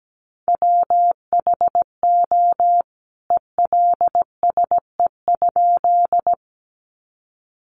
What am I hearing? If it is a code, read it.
Morse code, heard as a keyed tone, about 17 wpm, 700 Hz. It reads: WHO ELSE?
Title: A Sound Of Thunder 17wpm